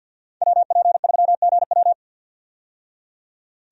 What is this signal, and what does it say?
Morse code, heard as a keyed tone, about 50 wpm, 700 Hz. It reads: WP3GW
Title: USA Call Signs - 5 Characters - 50wpm